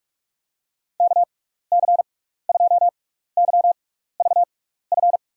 Morse code, heard as a keyed tone, about 44 wpm, spaced as for 15 wpm, 700 Hz. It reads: KC2YVF
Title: USA Call Signs - 6x Character Spacing - 45wpm